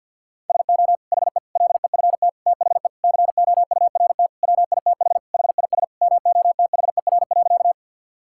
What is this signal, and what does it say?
Morse code, heard as a keyed tone, about 50 wpm, 700 Hz. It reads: SO HE LEFT THE COURT WITH HIS MOTHER.
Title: Sentences from Top 1600 Words 50wpm